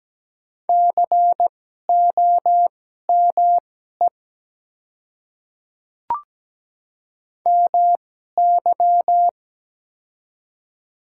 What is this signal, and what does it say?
Morse code, heard as a keyed tone, about 17 wpm, spaced as for 12 wpm, 690 Hz. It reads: COME MY
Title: 2x Character Spacing - Top 100 Words - 17wpm